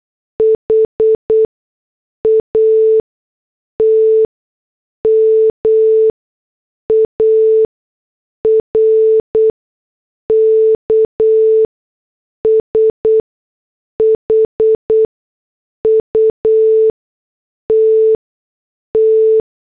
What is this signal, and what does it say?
Morse code, heard as a keyed tone, about 8 wpm, spaced as for 6 wpm, 440 Hz. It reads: HAT MARK SHUTT
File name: morse / headlines